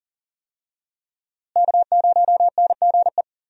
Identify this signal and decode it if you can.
Morse code, heard as a keyed tone, about 39 wpm, 700 Hz. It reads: K0NGE
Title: USA Call Signs - 5 Characters - Repeated 2x - 40wpm